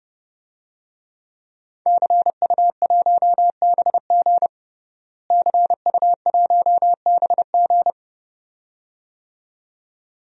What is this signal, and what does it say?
Morse code, heard as a keyed tone, about 30 wpm, 700 Hz. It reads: CU1BG CU1BG
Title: DXCC-100 Call Signs - 5 Characters - Repeated 2x - 30wpm